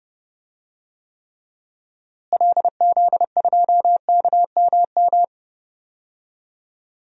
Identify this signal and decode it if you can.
Morse code, heard as a keyed tone, about 30 wpm, 700 Hz. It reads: LZ2KMM
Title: DXCC-100 Call Signs - 6 Characters - Repeated 2x - 30wpm